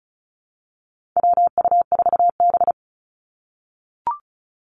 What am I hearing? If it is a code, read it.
Morse code, heard as a keyed tone, about 36 wpm, 700 Hz. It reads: WU4B
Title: Call Signs - 35wpm